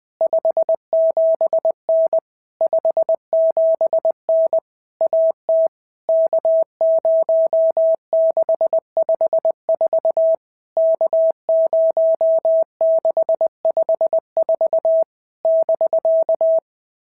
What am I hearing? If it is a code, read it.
Morse code, heard as a keyed tone, about 20 wpm, 650 Hz. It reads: 57N 57N AT K0654 K0654 <BK>
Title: POTA QSO Practice - 20wpm